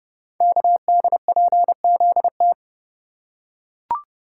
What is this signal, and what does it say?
Morse code, heard as a keyed tone, about 30 wpm, 700 Hz. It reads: KDPZT
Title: Code Groups - 5 Letters - 30wpm